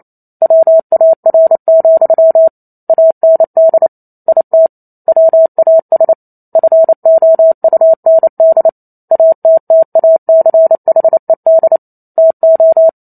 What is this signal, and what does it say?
Morse code, heard as a keyed tone, about 28 wpm, 650 Hz. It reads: WAR, AND IT WAS FOUND ATTACHED TO